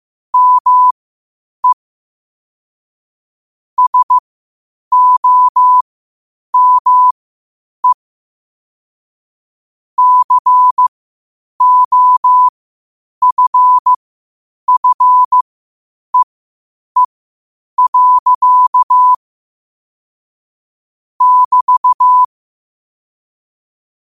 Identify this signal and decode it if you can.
Morse code, heard as a keyed tone, about 15 wpm, spaced as for 8 wpm, 1000 Hz. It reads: ME SOME COFFEE. <BT>
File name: qotdfiles / 8wpm